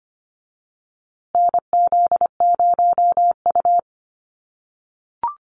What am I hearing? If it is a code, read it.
Morse code, heard as a keyed tone, about 25 wpm, 700 Hz. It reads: NZ0U
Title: USA Call Signs Repeated 3x 25wpm